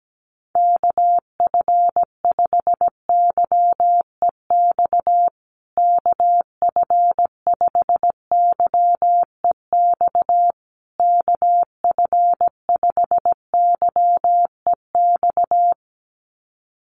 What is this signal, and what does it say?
Morse code, heard as a keyed tone, about 17 wpm, 700 Hz. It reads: KF5YEX KF5YEX KF5YEX
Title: USA Call Signs Repeated 3x 17wpm